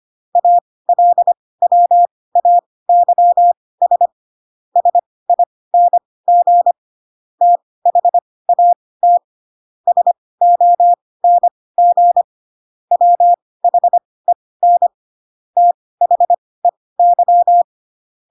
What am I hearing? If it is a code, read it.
Morse code, heard as a keyed tone, about 25 wpm, spaced as for 18 wpm, 700 Hz. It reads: ALWAYS SING THAT SONG WHEN THEY